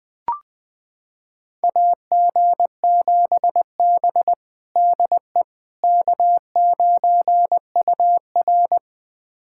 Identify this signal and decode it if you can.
Morse code, heard as a keyed tone, about 20 wpm, 700 Hz. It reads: AG7B DE K9UR
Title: USA Callsign DE Callsign - 4 Characters - 20wpm